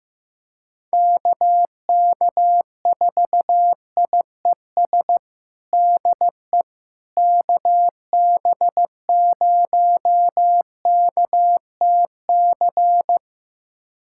Morse code, heard as a keyed tone, about 15 wpm, 700 Hz. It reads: KK4IES DE KB0KTC
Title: USA Callsign DE Callsign 15wpm